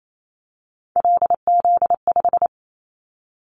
Morse code, heard as a keyed tone, about 28 wpm, 700 Hz. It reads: LZ5
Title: Callsign Prefixes for Top 50 DXCC - 28wpm